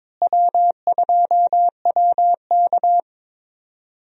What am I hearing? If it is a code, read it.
Morse code, heard as a keyed tone, about 22 wpm, 700 Hz. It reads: W2WK
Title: USA Call Signs - 4 Characters - Repeated 3x - 22wpm